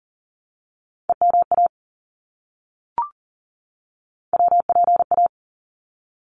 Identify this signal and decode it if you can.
Morse code, heard as a keyed tone, about 40 wpm, 700 Hz. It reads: EMA WPA
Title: ARRL Sections 40wpm